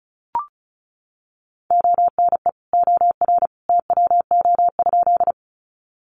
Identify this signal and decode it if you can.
Morse code, heard as a keyed tone, about 34 wpm, 700 Hz. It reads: ONE OR TWO?
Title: Sentences from Top 100 Words 35wpm